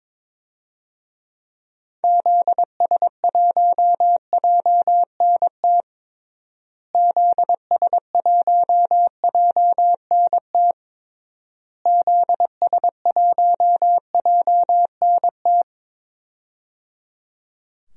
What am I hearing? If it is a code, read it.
Morse code, heard as a keyed tone, about 22 wpm, 700 Hz. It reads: ZS1JNT ZS1JNT ZS1JNT
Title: DXCC-100 Call Signs - 6 Characters - Repeated 3x - 22wpm